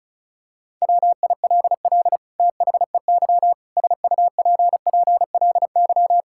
Morse code, heard as a keyed tone, about 36 wpm, 700 Hz. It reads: WILL THEY SUPPLY
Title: Sentences from Top 1000 Words - Review Part 5 - 35wpm